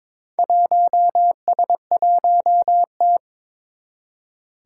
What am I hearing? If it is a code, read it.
Morse code, heard as a keyed tone, about 22 wpm, 700 Hz. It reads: 1S1T
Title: Four Challenging Letter-Number Combinations 22wpm